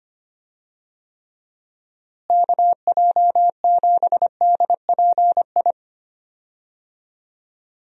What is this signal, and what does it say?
Morse code, heard as a keyed tone, about 25 wpm, 700 Hz. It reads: KJ7DPI
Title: USA Call Signs - 6 Characters - Repeated 3x - 25wpm